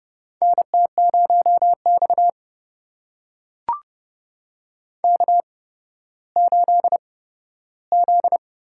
Morse code, heard as a keyed tone, about 30 wpm, spaced as for 14 wpm, 700 Hz. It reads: NT0X K8Z
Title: USA Call Signs - 4 Characters - 8x Character Spacing - 30wpm